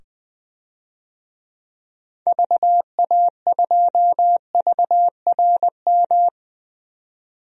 Morse code, heard as a keyed tone, about 20 wpm, 700 Hz. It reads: VA2VRM
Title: DXCC-100 Call Signs - 6 Characters - 20wpm